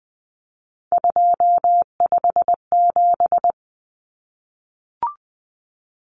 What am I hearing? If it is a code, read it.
Morse code, heard as a keyed tone, about 20 wpm, 700 Hz. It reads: 257
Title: Common Cut Numbers - 3-Digit 20wpm